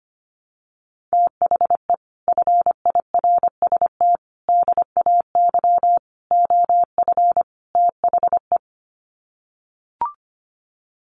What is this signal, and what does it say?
Morse code, heard as a keyed tone, about 25 wpm, 700 Hz. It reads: THE FIRST DAY OF THE
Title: Sets of 5 Words Encore2 25wpm